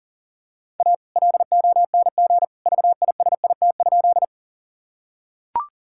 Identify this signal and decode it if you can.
Morse code, heard as a keyed tone, about 40 wpm, 700 Hz. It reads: A LONG VISIT?
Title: Sentences from Top 1000 Words - Review Part 6 - 40wpm